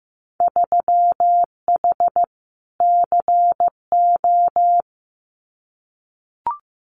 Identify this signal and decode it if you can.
Morse code, heard as a keyed tone, about 15 wpm, 700 Hz. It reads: 3H CO
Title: Winter Field Day Exchanges 15wpm